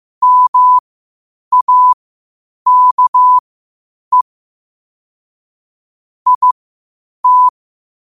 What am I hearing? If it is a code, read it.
Morse code, heard as a keyed tone, about 15 wpm, spaced as for 8 wpm, 1000 Hz. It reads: MAKE IT
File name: 8wpm